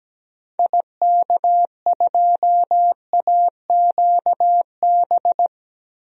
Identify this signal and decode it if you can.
Morse code, heard as a keyed tone, about 17 wpm, 700 Hz. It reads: IK2AQB
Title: DXCC-100 Call Signs - 6 Characters - 17wpm